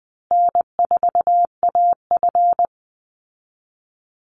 Morse code, heard as a keyed tone, about 20 wpm, 700 Hz. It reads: N4AF
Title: Call Signs Repeated 2x - 20wpm